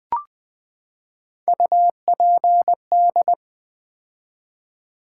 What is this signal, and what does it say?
Morse code, heard as a keyed tone, about 20 wpm, 690 Hz. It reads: UPD
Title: Build A Word - 3 Letters At A Time - 6-Letter Words - 20wpm